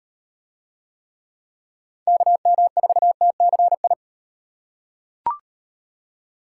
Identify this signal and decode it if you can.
Morse code, heard as a keyed tone, about 38 wpm, 700 Hz. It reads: KM4TCI